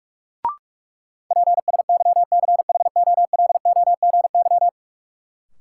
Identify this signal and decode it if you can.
Morse code, heard as a keyed tone, about 42 wpm, 700 Hz. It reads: PSYCHOLOGY
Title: Sentences from Top 1400 Words - New Word Drill Alt - 45wpm